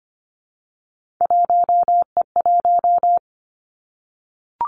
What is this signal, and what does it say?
Morse code, heard as a keyed tone, about 25 wpm, 700 Hz. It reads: 1E1
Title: Three Challenging Letter-Number Combinations 25wpm